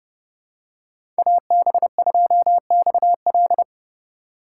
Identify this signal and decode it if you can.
Morse code, heard as a keyed tone, about 31 wpm, 700 Hz. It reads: AB2XL